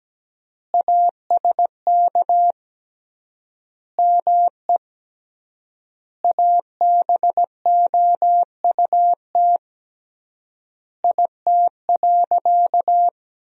Morse code, heard as a keyed tone, about 17 wpm, 700 Hz. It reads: ASK ME ABOUT IT.
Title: 3x Word-Spacing - Sentences from Top 200 Words - 17wpm